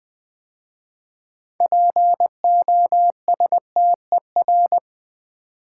Morse code, heard as a keyed tone, about 20 wpm, 700 Hz. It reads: POSTER
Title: 6 Letter Words - First 3 Letters Given - 20wpm